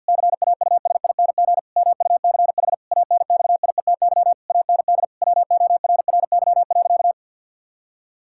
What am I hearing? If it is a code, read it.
Morse code, heard as a keyed tone, about 50 wpm, 700 Hz. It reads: CAUSING MUCH ANXIETY AND WORRY.
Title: Pangram Challenge v2 - 50wpm